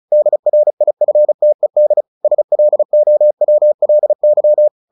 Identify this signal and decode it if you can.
Morse code, heard as a keyed tone, about 34 wpm, 600 Hz. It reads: DRIFTED SLOWLY
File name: the old man and the sea 2 35wpm